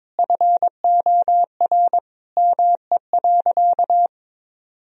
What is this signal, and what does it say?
Morse code, heard as a keyed tone, about 22 wpm, 700 Hz. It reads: FOR ME.
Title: Sentences from Top 1000 Words - Review Part 3 - 22wpm